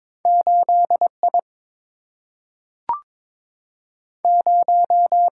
Morse code, heard as a keyed tone, about 22 wpm, 700 Hz. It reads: 8 I 0
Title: Single-Letter Single-Number 22wpm